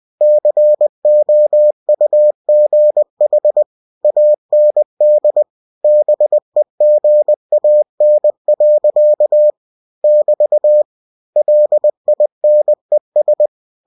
Morse code, heard as a keyed tone, about 20 wpm, 600 Hz. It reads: COUGH AND BEGAN. <BT> LINES